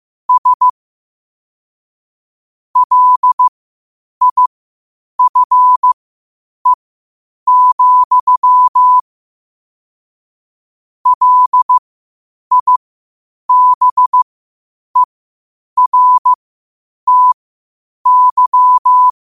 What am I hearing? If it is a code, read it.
Morse code, heard as a keyed tone, about 15 wpm, spaced as for 8 wpm, 1000 Hz. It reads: S LIFE, LIBERTY
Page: New quotes every day in morse code at 8 Words per minute.